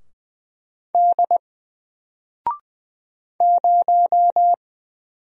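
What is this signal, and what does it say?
Morse code, heard as a keyed tone, about 20 wpm, 700 Hz. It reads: D 0
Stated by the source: Single letter and number Morse drill at 20 wpm, from N4LCD.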